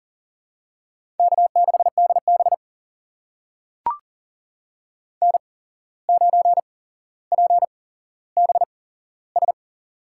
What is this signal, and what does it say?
Morse code, heard as a keyed tone, about 40 wpm, spaced as for 16 wpm, 700 Hz. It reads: K6DB N9PBS
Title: USA Call Signs - 8x Character Spacing - 40wpm